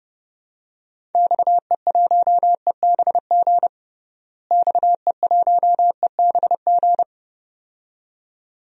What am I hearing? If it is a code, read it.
Morse code, heard as a keyed tone, about 31 wpm, 700 Hz. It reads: XE1EBG XE1EBG
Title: DXCC-100 Call Signs - 6 Characters - Repeated 2x - 30wpm